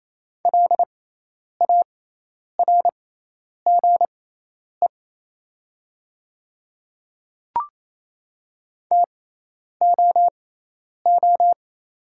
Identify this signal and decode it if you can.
Morse code, heard as a keyed tone, about 28 wpm, spaced as for 10 wpm, 700 Hz. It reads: LARGE TOO
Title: 6x Character Spacing - Top 500 Words - 28wpm